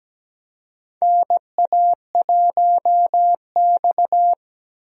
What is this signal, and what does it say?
Morse code, heard as a keyed tone, about 17 wpm, 700 Hz. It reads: NA1X
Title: USA Call Signs - 4 Characters - Repeated 2x - 17wpm